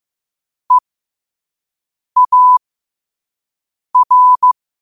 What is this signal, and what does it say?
Morse code, heard as a keyed tone, about 15 wpm, spaced as for 5 wpm, 1000 Hz. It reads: EAR
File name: 5wpm